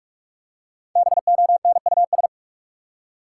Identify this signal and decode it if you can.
Morse code, heard as a keyed tone, about 43 wpm, 700 Hz. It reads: BONUS
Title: Five Letter Words - 45wpm